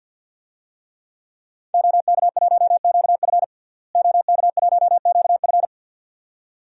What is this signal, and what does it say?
Morse code, heard as a keyed tone, about 53 wpm, 700 Hz. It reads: OK1QF OK1QF
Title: DXCC-100 Call Signs - 5 Characters - Repeated 2x - 50wpm